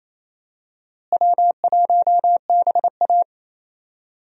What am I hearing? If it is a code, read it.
Morse code, heard as a keyed tone, about 28 wpm, 700 Hz. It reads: W1BA
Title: USA Call Signs - 4 Characters - Repeated 2x - 28wpm